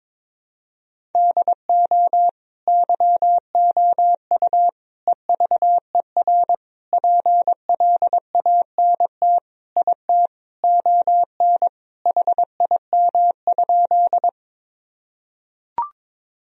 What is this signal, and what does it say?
Morse code, heard as a keyed tone, about 22 wpm, 700 Hz. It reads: DO YOU EVER PLANT IT ON HIM?
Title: Sentences from Top 1000 Words - Review Part 1 - 22wpm